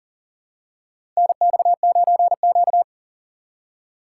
Morse code, heard as a keyed tone, about 40 wpm, 700 Hz. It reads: NX9Q
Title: USA Call Signs - 4 Characters - 3x Character Spacing - 40wpm